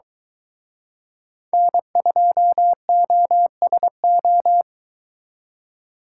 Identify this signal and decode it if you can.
Morse code, heard as a keyed tone, about 24 wpm, 700 Hz. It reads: N2OSO